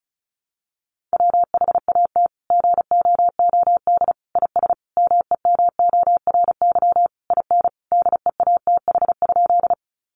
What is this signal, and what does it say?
Morse code, heard as a keyed tone, about 34 wpm, 700 Hz. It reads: WHAT GOOD IS MEMORY IN DEATH?
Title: Sentences from Top 1300 Words 35wpm